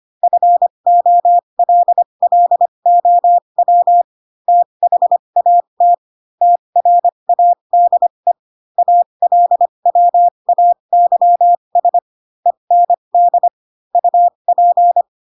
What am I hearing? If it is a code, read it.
Morse code, heard as a keyed tone, about 25 wpm, spaced as for 22 wpm, 700 Hz. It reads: FOLLOW THAT TRADE ALWAYS END UP
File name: Pinocchio / Pinocchio ch4 18 25 ik1hge